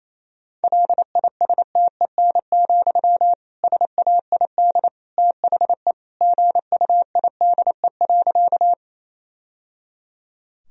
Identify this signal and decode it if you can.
Morse code, heard as a keyed tone, about 28 wpm, 700 Hz. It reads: LISTEN, SAID THE GUIDE.
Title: Sentences from Top 1000 Words - Review Part 3 - 28wpm